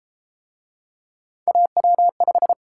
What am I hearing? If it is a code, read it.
Morse code, heard as a keyed tone, about 28 wpm, 700 Hz. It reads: AW5
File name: Speed-Racing - Three Letter-Number Combinations - 22wpm